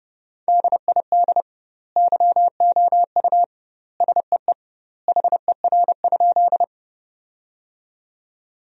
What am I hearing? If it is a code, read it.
Morse code, heard as a keyed tone, about 32 wpm, 700 Hz. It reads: DID YOU SEE HER?
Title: 2x Word-Spacing - Sentences from Top 100 Words - 30wpm